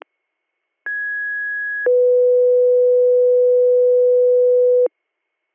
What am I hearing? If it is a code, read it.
Two-tone page, heard as a one-second tone, then a three-second tone, about 1650 then 500 Hz. It is heard from 0.9 to 4.9 s.